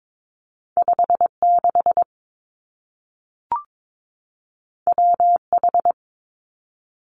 Morse code, heard as a keyed tone, about 22 wpm, 700 Hz. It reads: 5 6 W H
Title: Two Challenging Letter-Number Combinations 22wpm